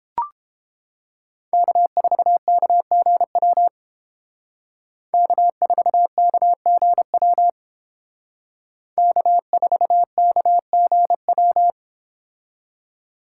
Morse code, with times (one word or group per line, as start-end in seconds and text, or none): 1.5-3.7 K4KGW
5.1-7.5 K4KGW
9.0-11.7 K4KGW